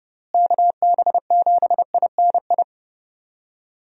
Morse code, heard as a keyed tone, about 30 wpm, 700 Hz. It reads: KB7INI
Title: USA Call Signs - 6 Characters - Repeated 2x - 30wpm